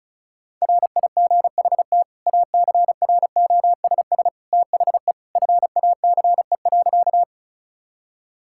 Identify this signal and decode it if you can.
Morse code, heard as a keyed tone, about 36 wpm, 700 Hz. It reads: RIGHT ACROSS THE FACE.
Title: Sentences from Top 1000 Words - Review Part 5 - 35wpm